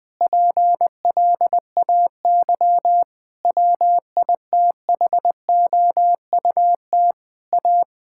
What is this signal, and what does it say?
Morse code, heard as a keyed tone, about 20 wpm, 700 Hz. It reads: PLAY WITHOUT A
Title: Sentences from Top 1000 Words - Review Part 6 - 20wpm